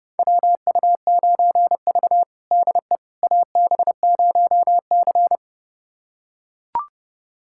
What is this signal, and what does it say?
Morse code, heard as a keyed tone, about 30 wpm, 700 Hz. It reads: WU9V DE AB0C
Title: USA Callsign DE Callsign - 4 Characters - 30wpm